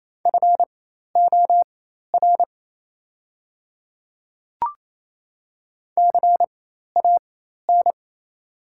Morse code, heard as a keyed tone, about 28 wpm, spaced as for 13 wpm, 690 Hz. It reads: FOR CAN
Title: 4x Character Spacing - Top 200 Words - 28wpm